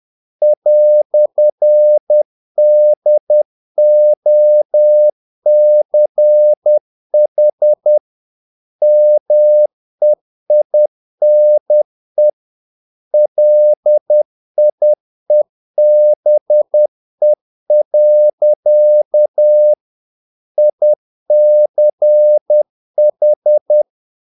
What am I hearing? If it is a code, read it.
Morse code, heard as a keyed tone, about 10 wpm, 600 Hz. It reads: "DOCH MEINE LIEBE. ICH
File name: dk5ke / Humor-50BpM